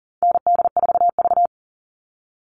Morse code, heard as a keyed tone, about 40 wpm, 700 Hz. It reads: ND4V
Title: Call Signs Repeated 2x - 40wpm